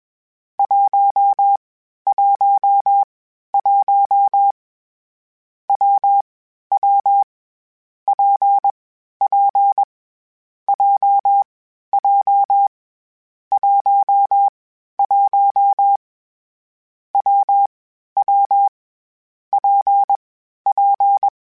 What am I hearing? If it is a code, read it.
Morse code, heard as a keyed tone, about 21 wpm, 800 Hz. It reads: 1 1 1 W W P P J J 1 1 W W P P
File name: Difficult Combos / Difficult Combos-mp3s / WPJ WPJ1 @20